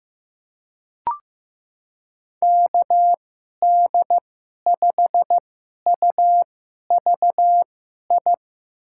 Morse code, heard as a keyed tone, about 15 wpm, spaced as for 11 wpm, 700 Hz. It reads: KD5UVI
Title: USA Call Signs - 6 Characters - 2x Character Spacing - 15wpm